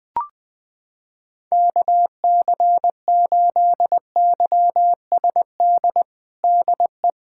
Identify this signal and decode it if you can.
Morse code, heard as a keyed tone, about 20 wpm, 700 Hz. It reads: KC8YSD DE
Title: USA Callsign DE Callsign - 6 Characters - 20wpm